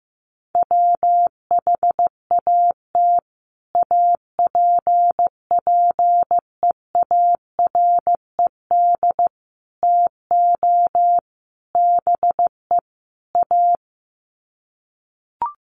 Morse code, heard as a keyed tone, about 15 wpm, 700 Hz. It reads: WHAT APPEARED TO BE A
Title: Sets of 5 Words Encore2 15wpm